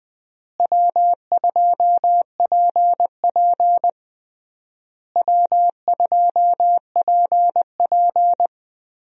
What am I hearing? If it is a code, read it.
Morse code, heard as a keyed tone, about 21 wpm, 700 Hz. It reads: W2PP W2PP
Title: USA Call Signs - 4 Characters - Repeated 3x - 20wpm